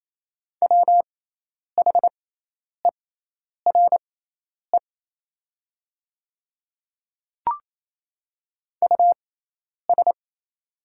Morse code, heard as a keyed tone, about 28 wpm, spaced as for 10 wpm, 700 Hz. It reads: WHERE US
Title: 6x Character Spacing - Top 200 Words - 28wpm